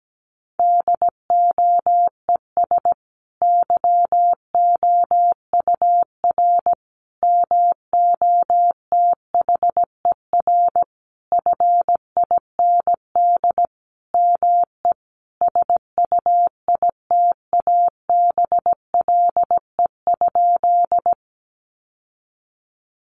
Morse code, heard as a keyed tone, about 17 wpm, 700 Hz. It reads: DOES YOUR MOTHER FIND ME SUITABLE?
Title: Sentences from Top 1400 Words 17wpm